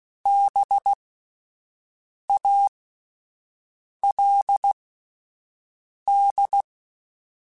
Morse code, heard as a keyed tone, about 16 wpm, spaced as for 5 wpm, 800 Hz. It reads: BALD
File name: Spaced Alphas / Spaced Alphas mp3s / Bs sp @10